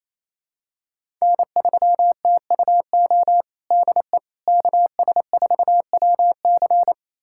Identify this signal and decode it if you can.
Morse code, heard as a keyed tone, about 27 wpm, 700 Hz. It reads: N3TUO DE KS4WC
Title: USA Callsign DE Callsign - 5 Characters - 28wpm